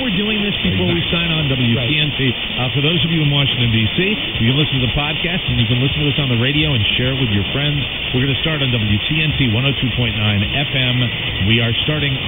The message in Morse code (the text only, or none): none